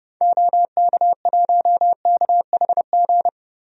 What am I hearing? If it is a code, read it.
Morse code, heard as a keyed tone, about 30 wpm, 700 Hz. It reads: OK1KHG
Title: DXCC-100 Call Signs - 6 Characters - 30wpm